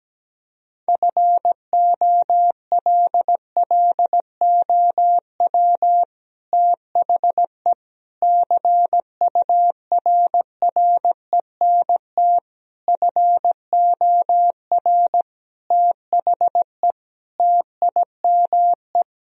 Morse code, heard as a keyed tone, about 17 wpm, 700 Hz. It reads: FOLLOW THE CURRENT FOR THE TIME
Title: Sentences from Top 1000 Words - Review Part 4 - 17wpm